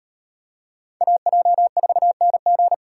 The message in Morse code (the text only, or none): AJ4NG